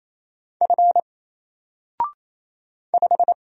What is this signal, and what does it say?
Morse code, heard as a keyed tone, about 27 wpm, 700 Hz. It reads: F 5
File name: Single-Letter-Number-28wpm